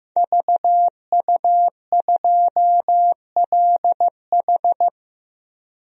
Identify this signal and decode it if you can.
Morse code, heard as a keyed tone, about 15 wpm, 700 Hz. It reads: VU2LH
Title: DXCC-100 Call Signs - 5 Characters - 15wpm